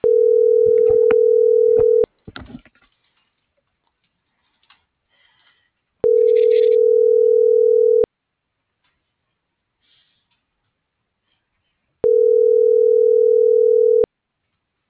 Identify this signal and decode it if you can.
Ringback tone (North America), 440 + 480 Hz.